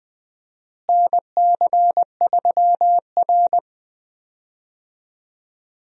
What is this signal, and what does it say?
Morse code, heard as a keyed tone, about 20 wpm, 700 Hz. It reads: NC3R
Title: USA Call Signs - 4 Characters - Repeated 2x - 20wpm